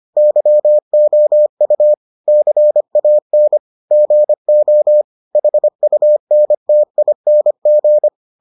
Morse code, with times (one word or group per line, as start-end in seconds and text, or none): 0.2-1.9 YOU
2.3-3.6 CAN
3.9-5.0 GO
5.3-8.1 HUNTING